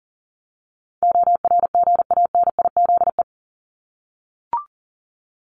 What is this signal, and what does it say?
Morse code, heard as a keyed tone, about 40 wpm, 700 Hz. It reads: ORGANIZE
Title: Top 1751-2000 Words 40wpm